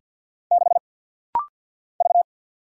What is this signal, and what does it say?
Morse code, heard as a keyed tone, about 49 wpm, 700 Hz. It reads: 6 V
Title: Single Letter-Number Alt - Rapid-Fire 50wpm